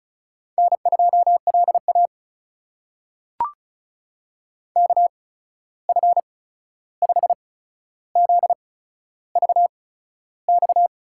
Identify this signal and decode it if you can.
Morse code, heard as a keyed tone, about 36 wpm, spaced as for 13 wpm, 700 Hz. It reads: N2LA KF5ZVX
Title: USA Call Signs - 8x Character Spacing - 35wpm